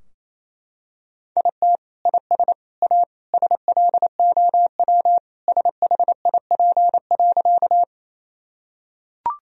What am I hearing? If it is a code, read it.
Morse code, heard as a keyed tone, about 28 wpm, 700 Hz. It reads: IT IS A SLOW SHIP.
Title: Sentences from Top 1000 Words - Review Part 2 - 28wpm